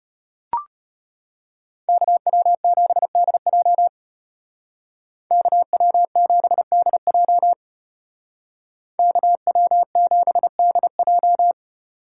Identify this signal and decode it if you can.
Morse code, heard as a keyed tone, about 33 wpm, 700 Hz. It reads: KW7DJ KW7DJ KW7DJ